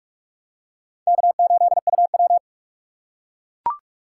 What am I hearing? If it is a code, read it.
Morse code, heard as a keyed tone, about 44 wpm, 700 Hz. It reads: K8UW
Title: USA Call Signs - 4 Characters - 2x Character Spacing - 45wpm